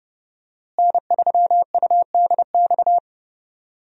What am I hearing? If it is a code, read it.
Morse code, heard as a keyed tone, about 30 wpm, 700 Hz. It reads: N3UDX
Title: USA Call Signs - 5 Characters - Repeated 3x - 30wpm